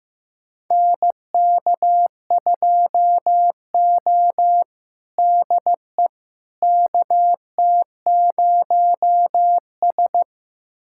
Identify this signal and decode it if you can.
Morse code, heard as a keyed tone, about 15 wpm, 700 Hz. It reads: NK2O DE KT0S
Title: USA Callsign DE Callsign - 4 Characters - 15wpm